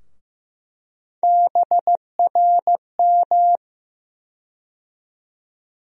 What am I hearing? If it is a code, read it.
Morse code, heard as a keyed tone, about 15 wpm, 700 Hz. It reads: BRM
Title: USA Call Sign Postfixes - 3 Letters 15wpm